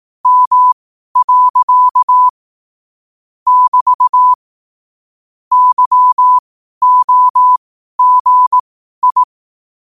Morse code, heard as a keyed tone, about 18 wpm, spaced as for 12 wpm, 1000 Hz. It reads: M. <BT> YOGI
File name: qotdfiles / 12wpm